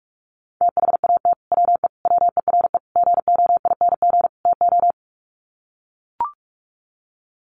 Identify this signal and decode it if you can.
Morse code, heard as a keyed tone, about 44 wpm, 700 Hz. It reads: THAT WE WERE GOING TO
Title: Sets of 5 Words Encore2 45wpm